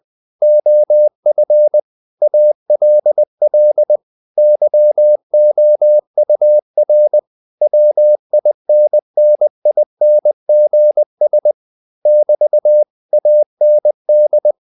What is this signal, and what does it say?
Morse code, heard as a keyed tone, about 20 wpm, 600 Hz. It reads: OF ALL YOUR WINNINGS <BT> AND
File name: CW / Practice / If20